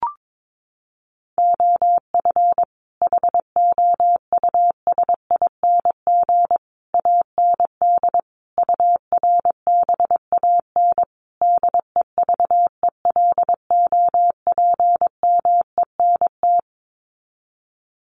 Morse code, heard as a keyed tone, about 22 wpm, 700 Hz. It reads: OF HOUSING AND URBAN DEVELOPMENT
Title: Sets of 5 Words Encore2 22wpm